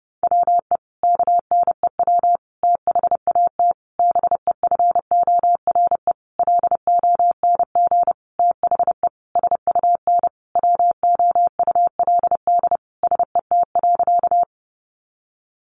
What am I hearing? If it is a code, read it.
Morse code, heard as a keyed tone, about 30 wpm, 700 Hz. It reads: WE KNEW THAT BEFORE LONG THE SUN WOULD SET.
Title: Sentences from Top 400 Words 30wpm